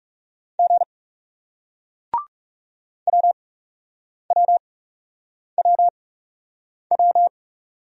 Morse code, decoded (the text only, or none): G WWWW